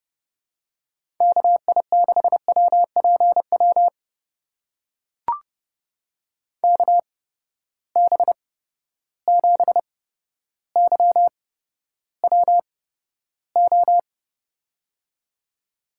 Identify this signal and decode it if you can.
Morse code, heard as a keyed tone, about 30 wpm, spaced as for 13 wpm, 700 Hz. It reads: KI6WPW KB7YWO
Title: USA Call Signs - 6 Characters - 8x Character Spacing - 30wpm